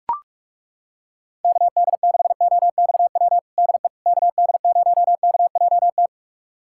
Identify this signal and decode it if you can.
Morse code, heard as a keyed tone, about 42 wpm, 700 Hz. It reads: KD6OXW DE KD0KJT
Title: USA Callsign DE Callsign - 6 Characters - 45wpm